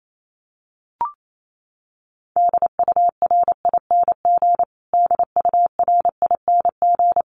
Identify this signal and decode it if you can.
Morse code, heard as a keyed tone, about 28 wpm, 700 Hz. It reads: DURING DURING
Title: Top 500 Words Repeated 28wpm